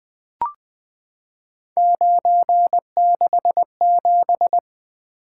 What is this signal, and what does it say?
Morse code, heard as a keyed tone, about 20 wpm, 700 Hz. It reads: 967
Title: Three Letter-Number Combinations - 20wpm